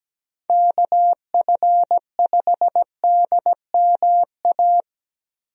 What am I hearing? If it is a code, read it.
Morse code, heard as a keyed tone, about 17 wpm, 700 Hz. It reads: KF5DMA
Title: USA Call Signs - 6 Characters - Repeated 3x - 17wpm